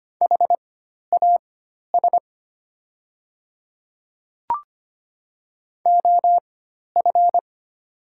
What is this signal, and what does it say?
Morse code, heard as a keyed tone, about 25 wpm, spaced as for 12 wpm, 700 Hz. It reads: HAS OF
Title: 4x Character Spacing - Top 300 Words - 25wpm